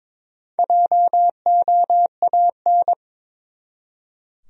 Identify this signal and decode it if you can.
Morse code, heard as a keyed tone, about 22 wpm, 700 Hz. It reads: JOAN
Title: USA Common Names - Max 5 Letters - 22wpm